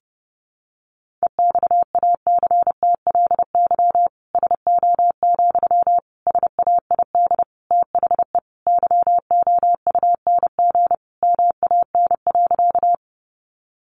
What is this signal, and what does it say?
Morse code, heard as a keyed tone, about 28 wpm, 700 Hz. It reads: EXACTLY SO, SAID THE YOUNG MAN.
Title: Sentences from Top 900 Words 30wpm